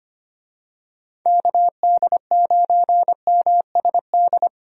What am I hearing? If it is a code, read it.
Morse code, heard as a keyed tone, about 25 wpm, 700 Hz. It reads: KD9MSD
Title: USA Call Signs - 6 Characters - Repeated 2x - 25wpm